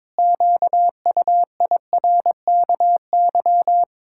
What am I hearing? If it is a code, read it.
Morse code, heard as a keyed tone, about 22 wpm, 700 Hz. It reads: QUIRKY
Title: Pangram Challenge v2 - 22wpm